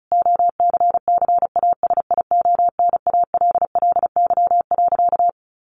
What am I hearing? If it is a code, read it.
Morse code, heard as a keyed tone, about 34 wpm, 700 Hz. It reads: OCCASIONALLY.
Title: Sentences from Top 1300 Words 35wpm